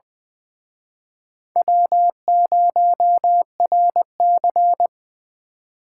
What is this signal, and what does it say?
Morse code, heard as a keyed tone, about 23 wpm, 700 Hz. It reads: W0RC
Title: Speed Racing - USA 1x2 Call Signs 20wpm